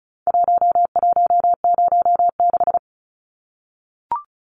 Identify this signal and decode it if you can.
Morse code, heard as a keyed tone, about 34 wpm, 700 Hz. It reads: 1106
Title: Numbers - Contest Style - 4-Digit 35wpm